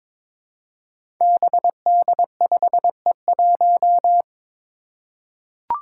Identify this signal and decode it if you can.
Morse code, heard as a keyed tone, about 22 wpm, 700 Hz. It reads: BD5E1
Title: Five Challenging Letter-Number Combinations 22wpm